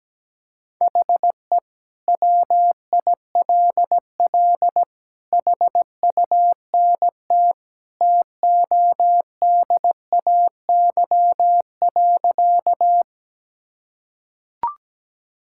Morse code, heard as a keyed tone, about 17 wpm, 700 Hz. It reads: HE WILL HUNT TODAY.
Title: Sentences from Top 1600 Words 17wpm